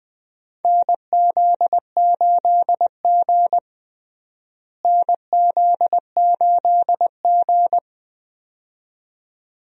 Morse code, heard as a keyed tone, about 21 wpm, 700 Hz. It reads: NZ8G NZ8G
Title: USA Call Signs - 4 Characters - Repeated 2x - 20wpm